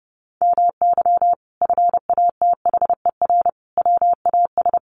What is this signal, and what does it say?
Morse code, heard as a keyed tone, about 30 wpm, 700 Hz. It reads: MY FATHER WAS
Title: Sentences from Top 1400 Words 30wpm